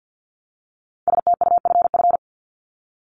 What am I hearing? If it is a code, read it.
Morse code, heard as a keyed tone, about 50 wpm, 700 Hz. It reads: STUFF
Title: Top 1251-1500 Words 50wpm